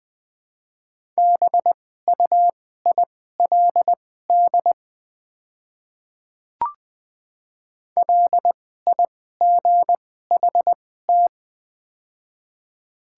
Morse code, heard as a keyed tone, about 20 wpm, spaced as for 15 wpm, 700 Hz. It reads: BUILD LIGHT
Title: 2x Character Spacing - Top 200 Words - 20wpm